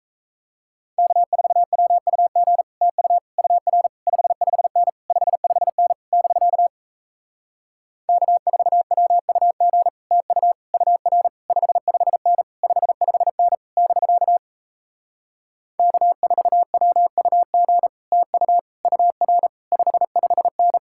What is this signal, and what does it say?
Morse code, heard as a keyed tone, about 37 wpm, 700 Hz. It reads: K4WUG TU UR 55N 55N <BK> K4WUG TU UR 55N 55N <BK> K4WUG TU UR 55N